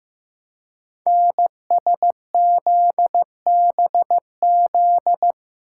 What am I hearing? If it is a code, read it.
Morse code, heard as a keyed tone, about 15 wpm, 700 Hz. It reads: NSZBZ
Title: Code Groups - 5 Letters - 15wpm